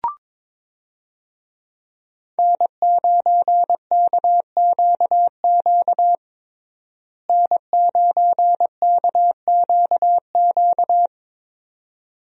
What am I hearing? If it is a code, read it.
Morse code, heard as a keyed tone, about 22 wpm, 700 Hz. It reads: N9KQQ N9KQQ
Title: USA Call Signs - 5 Characters - Repeated 2x - 22wpm